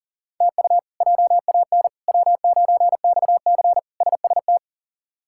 Morse code, heard as a keyed tone, about 38 wpm, 700 Hz. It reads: TU JAN W9XC SST
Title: K1USN SST QSO Practice - 40wpm